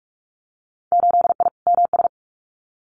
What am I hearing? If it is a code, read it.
Morse code, heard as a keyed tone, about 44 wpm, 700 Hz. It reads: 8I MS
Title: Winter Field Day Exchanges Encore 45wpm